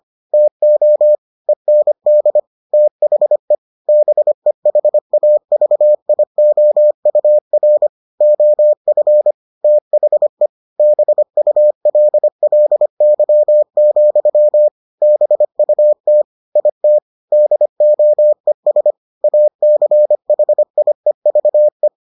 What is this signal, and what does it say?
Morse code, heard as a keyed tone, about 25 wpm, 600 Hz. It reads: TO END THE BEHAVIOUR OF THE BULLY, BUT IT DOES ACHIEVE